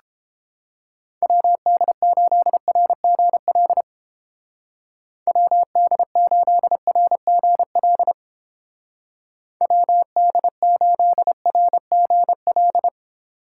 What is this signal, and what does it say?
Morse code, heard as a keyed tone, about 29 wpm, 700 Hz. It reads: WD8RGL WD8RGL WD8RGL